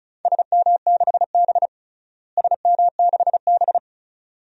Morse code, heard as a keyed tone, about 34 wpm, 700 Hz. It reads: SM6B SM6B
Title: DXCC-100 Call Signs - 4 Characters - Repeated 2x - 35wpm